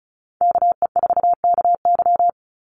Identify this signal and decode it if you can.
Morse code, heard as a keyed tone, about 34 wpm, 700 Hz. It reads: KE4KY
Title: Call Signs Repeated 2x - 35wpm